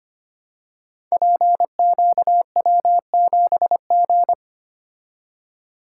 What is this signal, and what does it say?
Morse code, heard as a keyed tone, about 25 wpm, 690 Hz. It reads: PQW7G
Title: Code Groups - 5 Letter:Numbers - 25wpm